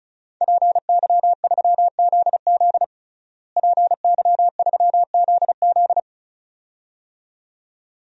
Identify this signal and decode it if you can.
Morse code, heard as a keyed tone, about 34 wpm, 700 Hz. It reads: PY3ZZ PY3ZZ
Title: DXCC-100 Call Signs - 5 Characters - Repeated 2x - 35wpm